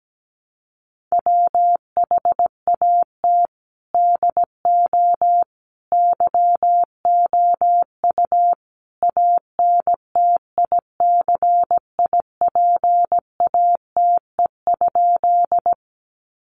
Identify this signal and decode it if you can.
Morse code, heard as a keyed tone, about 17 wpm, 700 Hz. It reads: WHAT DO YOU ANTICIPATE?
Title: Sentences from Top 1500 Words 17wpm